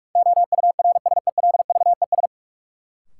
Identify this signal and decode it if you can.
Morse code, heard as a keyed tone, about 46 wpm, 700 Hz. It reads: OURSELVES
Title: Sentences from Top 1600 Words - New Word Drill - 45wpm